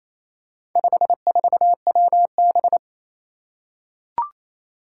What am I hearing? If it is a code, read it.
Morse code, heard as a keyed tone, about 27 wpm, 700 Hz. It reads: 54WB
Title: Four Challenging Letter-Number Combinations 28wpm